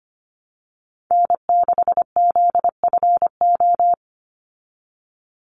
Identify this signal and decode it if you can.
Morse code, heard as a keyed tone, about 25 wpm, 700 Hz. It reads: N6ZFO
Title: Call Signs Repeated 2x - 25wpm